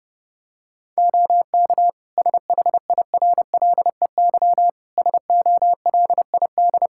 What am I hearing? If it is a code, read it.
Morse code, heard as a keyed tone, about 32 wpm, 700 Hz. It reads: OK SHIRLEY SOLID
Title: Practice QSO - Exchange 3 - More Info - 30wpm